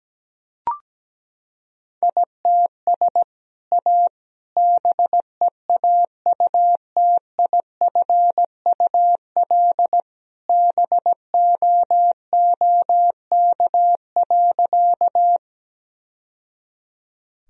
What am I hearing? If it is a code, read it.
Morse code, heard as a keyed tone, about 17 wpm, 700 Hz. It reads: ITS A BEAUTIFUL BOOK.
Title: Sentences from Top 1000 Words - Review Part 1 - 17wpm